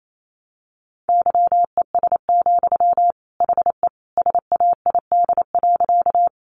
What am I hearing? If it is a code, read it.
Morse code, heard as a keyed tone, about 28 wpm, 700 Hz. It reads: YES, HE SAID.
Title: Sentences from Top 500 Words 28wpm